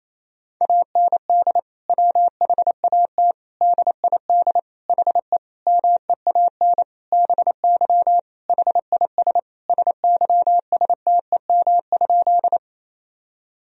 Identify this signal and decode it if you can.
Morse code, heard as a keyed tone, about 27 wpm, 700 Hz. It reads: AND WHAT DID HE MEAN BY HIS SYSTEM?
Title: Sentences from Top 1000 Words - Review Part 3 - 28wpm